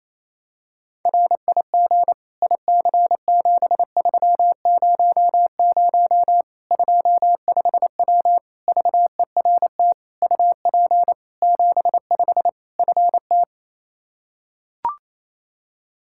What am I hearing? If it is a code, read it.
Morse code, heard as a keyed tone, about 27 wpm, 700 Hz. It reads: RIG IC7300 25W VERT UP 75 FT
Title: Practice QSO - Exchange 2 - Info - 28wpm